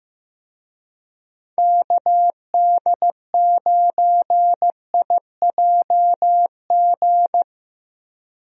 Morse code, heard as a keyed tone, about 15 wpm, 700 Hz. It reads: KD9IJG
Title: USA Call Signs - 6 Characters - Repeated 3x - 15wpm